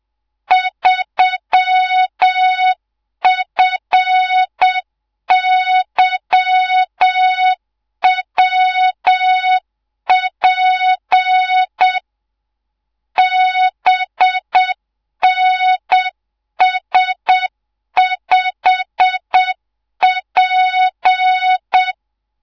Morse code, heard as a keyed tone, about 7 wpm, 3700 Hz, with noise. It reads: 3FYWP BNS5P
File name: Morze-HA2MN / hangp / 2 03 GYAKORLAT